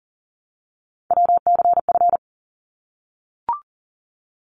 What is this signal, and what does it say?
Morse code, heard as a keyed tone, about 40 wpm, 700 Hz. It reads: WCF
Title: ARRL Sections 40wpm